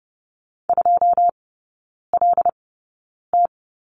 Morse code, heard as a keyed tone, about 30 wpm, spaced as for 9 wpm, 700 Hz. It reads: 2LT
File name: Lesson 19e - Letter w - Three Characters 30wpm